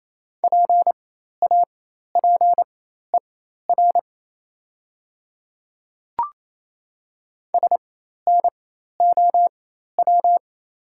Morse code, heard as a keyed tone, about 28 wpm, spaced as for 13 wpm, 690 Hz. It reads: PAPER SNOW
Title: 4x Character Spacing - Top 500 Words - 28wpm